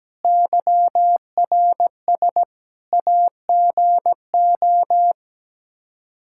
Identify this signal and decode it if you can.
Morse code, heard as a keyed tone, about 17 wpm, 700 Hz. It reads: YRS AGO
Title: K1USN SST QSO Practice 17wpm